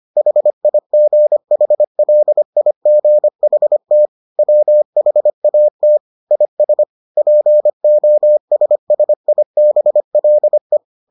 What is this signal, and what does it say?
Morse code, heard as a keyed tone, about 25 wpm, 600 Hz. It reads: HIGHLIGHT WHAT IS POSSIBLE